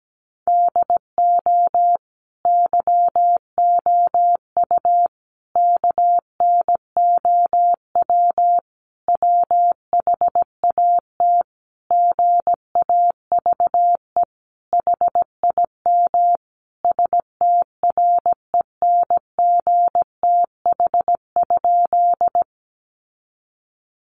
Morse code, heard as a keyed tone, about 17 wpm, 700 Hz. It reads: DO YOU KNOW WHAT GAVE HIM STRENGTH?
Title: Sentences from Top 1300 Words 17wpm